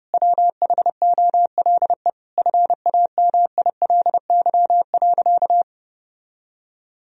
Morse code, heard as a keyed tone, about 28 wpm, 700 Hz. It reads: WHOLE FAMILY.
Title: Sentences from Top 1000 Words - Review Part 4 - 30wpm